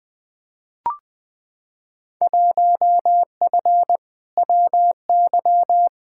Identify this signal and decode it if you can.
Morse code, heard as a keyed tone, about 20 wpm, 700 Hz. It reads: 1F WY
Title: Field Day Exchanges Encore - 20wpm